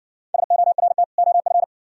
Morse code, heard as a keyed tone, about 59 wpm, 700 Hz. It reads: SORT OF